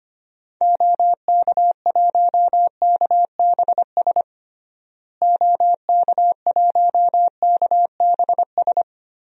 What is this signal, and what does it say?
Morse code, heard as a keyed tone, about 25 wpm, 700 Hz. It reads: OK1KBS OK1KBS
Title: DXCC-100 Call Signs - 6 Characters - Repeated 2x - 25wpm